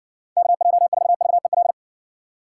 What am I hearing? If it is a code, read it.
Morse code, heard as a keyed tone, about 60 wpm, 700 Hz. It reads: KJ4VEL